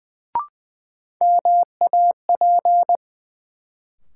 Morse code, heard as a keyed tone, about 20 wpm, 700 Hz. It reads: MAP
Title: Sentences from Top 400 Words - New Word Drill - 20wpm